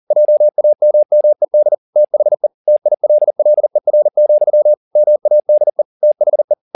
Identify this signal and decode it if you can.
Morse code, heard as a keyed tone, about 40 wpm, 600 Hz. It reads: JAMMED THE TILLER, MADE THE
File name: the old man and the sea 2 40wpm